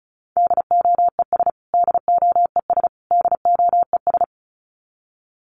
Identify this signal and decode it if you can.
Morse code, heard as a keyed tone, about 34 wpm, 700 Hz. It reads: DOES DOES DOES
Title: Top 500 Words Repeated 35wpm